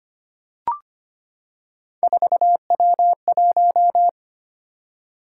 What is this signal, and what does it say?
Morse code, heard as a keyed tone, about 25 wpm, 700 Hz. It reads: 4W1
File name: Three Letter-Number Combinations - 25wpm